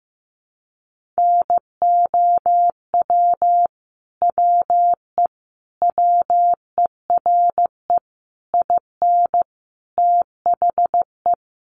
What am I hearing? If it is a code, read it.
Morse code, heard as a keyed tone, about 15 wpm, 700 Hz. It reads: NOW WE WERE IN THE
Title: Sentences from Top 1200 Words 15wpm